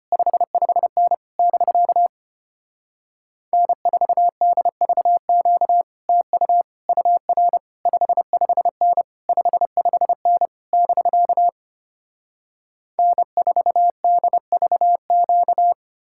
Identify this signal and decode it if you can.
Morse code, heard as a keyed tone, about 33 wpm, 700 Hz. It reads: 55N <BK> N4DVQ TU UR 55N 55N <BK> N4DVQ